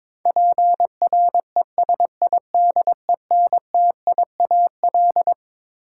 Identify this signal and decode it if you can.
Morse code, heard as a keyed tone, about 22 wpm, 700 Hz. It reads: PRESIDENTIAL
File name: Twelve Letter Words - 22wpm